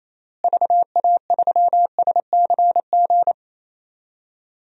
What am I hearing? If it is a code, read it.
Morse code, heard as a keyed tone, about 28 wpm, 700 Hz. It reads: VA3SCG
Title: DXCC-100 Call Signs - 6 Characters - Repeated 2x - 28wpm